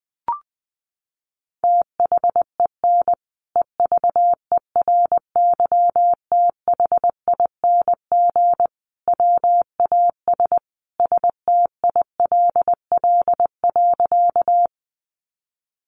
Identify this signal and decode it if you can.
Morse code, heard as a keyed tone, about 19 wpm, 700 Hz. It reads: THEN EVERYTHING WAS STILL.
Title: Sentences from Top 700 Words 20wpm